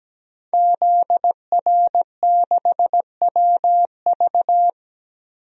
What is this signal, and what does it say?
Morse code, heard as a keyed tone, about 17 wpm, 700 Hz. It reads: ZR6WV
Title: DXCC-100 Call Signs - 5 Characters - 17wpm